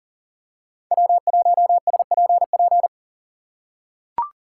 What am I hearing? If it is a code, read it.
Morse code, heard as a keyed tone, about 40 wpm, 700 Hz. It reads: W1SPP
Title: USA Call Signs - 2x Character Spacing - 40wpm